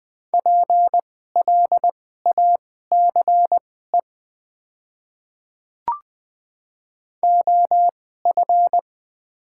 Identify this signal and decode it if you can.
Morse code, heard as a keyed tone, about 20 wpm, spaced as for 15 wpm, 700 Hz. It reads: PLACE OF
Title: 2x Character Spacing - Top 500 Words - 20wpm